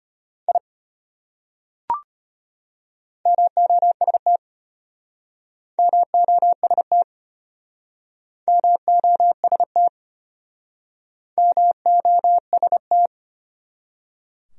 Morse code, heard as a keyed tone, about 33 wpm, 700 Hz. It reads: I MOST MOST MOST MOST